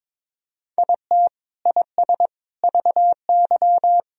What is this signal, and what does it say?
Morse code, heard as a keyed tone, about 22 wpm, 700 Hz. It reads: IT IS VY
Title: Sentences from Top 200 Words and Abbreviations 22wpm